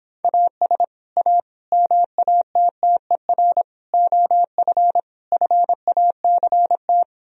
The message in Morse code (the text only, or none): AS A MATTER OF FACT